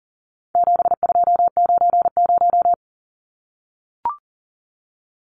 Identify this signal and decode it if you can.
Morse code, heard as a keyed tone, about 40 wpm, 700 Hz. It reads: 7290
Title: Numbers - 4-Digit 40wpm